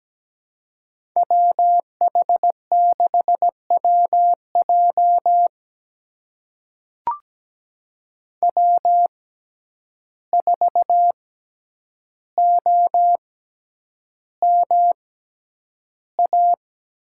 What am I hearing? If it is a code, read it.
Morse code, heard as a keyed tone, about 17 wpm, spaced as for 9 wpm, 700 Hz. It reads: WH6WJ W4OMA
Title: USA Call Signs - 5 Characters - 6x Character Spacing - 17wpm